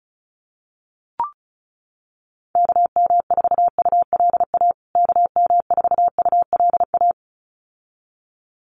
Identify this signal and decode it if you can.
Morse code, heard as a keyed tone, about 34 wpm, 700 Hz. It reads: KM4ULA KM4ULA
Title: USA Call Signs Repeated 2x 35wpm